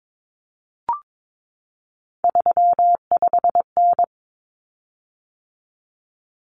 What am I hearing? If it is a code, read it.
Morse code, heard as a keyed tone, about 22 wpm, 700 Hz. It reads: 35N
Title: Common Cut Numbers - 3-Digit 22wpm